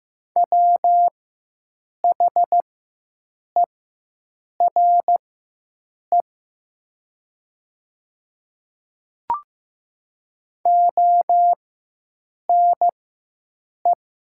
Morse code, heard as a keyed tone, about 15 wpm, spaced as for 7 wpm, 700 Hz. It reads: WHERE ONE
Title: 4x Character Spacing - Top 200 Words - 15wpm